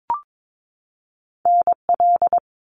Morse code, heard as a keyed tone, about 22 wpm, 700 Hz. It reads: NL
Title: Canadian Provinces and Territories 22wpm